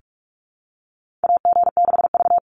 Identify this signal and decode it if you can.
Morse code, heard as a keyed tone, about 44 wpm, 690 Hz. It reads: AG6V
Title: NA SOTA Chaser Callsigns 45wpm